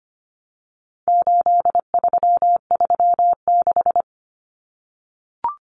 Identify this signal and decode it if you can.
Morse code, heard as a keyed tone, about 25 wpm, 700 Hz. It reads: 8336
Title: Numbers - 4-Digit 25wpm